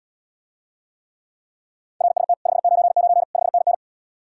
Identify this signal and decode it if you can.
Morse code, heard as a keyed tone, about 79 wpm, 700 Hz. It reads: RST 599 5NN